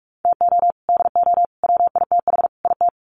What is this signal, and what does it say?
Morse code, heard as a keyed tone, about 44 wpm, 700 Hz. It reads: TO DO WITH IT
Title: Sets of 4 Words 45wpm